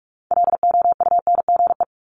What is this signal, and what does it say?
Morse code, heard as a keyed tone, about 46 wpm, 700 Hz. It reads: LOUNGE
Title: Multi-letter Phonemes 45wpm